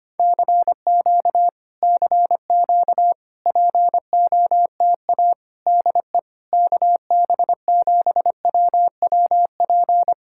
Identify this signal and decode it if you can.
Morse code, heard as a keyed tone, about 27 wpm, 700 Hz. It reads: CQ CQ POTA DE KB7WWP
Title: Speed-Racing Common QSO Phrases - 25wpm